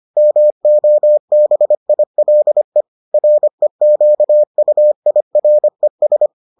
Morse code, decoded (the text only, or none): MOBILE REQUIRES